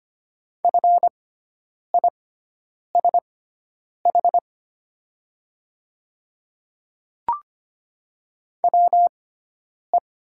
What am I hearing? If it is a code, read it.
Morse code, heard as a keyed tone, about 25 wpm, spaced as for 9 wpm, 700 Hz. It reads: FISH WE
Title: 6x Character Spacing - Top 300 Words - 25wpm